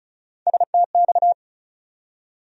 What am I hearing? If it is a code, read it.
Morse code, heard as a keyed tone, about 36 wpm, 700 Hz. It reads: STX
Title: USA Call Sign Postfixes - 3 Letters 35wpm